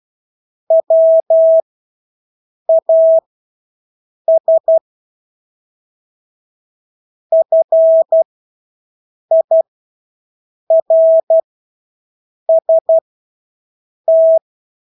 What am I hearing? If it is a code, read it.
Morse code, heard as a keyed tone, about 12 wpm, spaced as for 6 wpm, 650 Hz. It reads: WAS FIRST
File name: Engelske ord 12-6 wpm